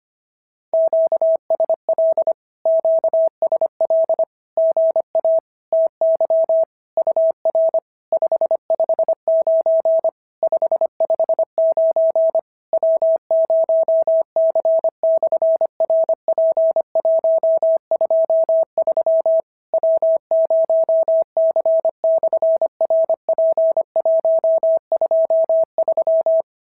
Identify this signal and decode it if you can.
Morse code, heard as a keyed tone, about 25 wpm, 650 Hz. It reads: QSL QSL GA TY UR 559 559 W0C/RP123 W0C/RP123
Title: SOTA QSO Practice 25wpm